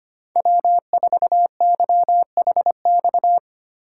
Morse code, heard as a keyed tone, about 28 wpm, 700 Hz. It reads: W4YHX
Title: Speed Racing - USA Call Signs 25wpm